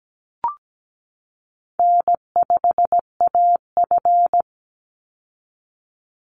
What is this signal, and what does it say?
Morse code, heard as a keyed tone, about 17 wpm, 700 Hz. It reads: N5AF
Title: Easy Call Signs Repeated 3x - 17wpm